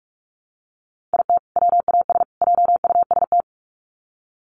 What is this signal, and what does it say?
Morse code, heard as a keyed tone, about 44 wpm, 700 Hz. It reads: IT WAS JUST
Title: Sets of 3 Words 45wpm